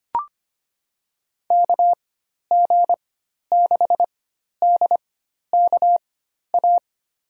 Morse code, heard as a keyed tone, about 25 wpm, spaced as for 12 wpm, 700 Hz. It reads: KG6DKA
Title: USA Call Signs - 6 Characters - 4x Character Spacing - 25wpm